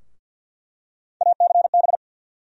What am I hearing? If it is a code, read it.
Morse code, heard as a keyed tone, about 49 wpm, 700 Hz. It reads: ACB
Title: USA Call Sign Postfixes - 3 Letters 50wpm